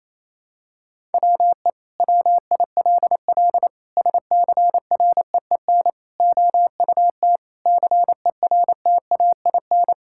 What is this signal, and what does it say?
Morse code, heard as a keyed tone, about 28 wpm, 700 Hz. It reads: WE WILL SCREEN OUT CERTAIN
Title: Sentences from Top 1000 Words - Review Part 6 - 28wpm